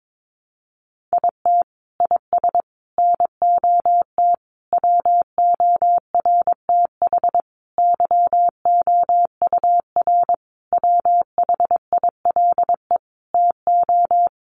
Morse code, heard as a keyed tone, about 22 wpm, 700 Hz. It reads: IT IS NOT WORTH YOUR WHILE TO
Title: Sentences from Top 800 Words 22wpm